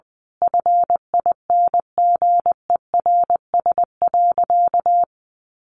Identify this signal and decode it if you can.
Morse code, heard as a keyed tone, about 20 wpm, 700 Hz. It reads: FINGERS.
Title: A Sound Of Thunder 20wpm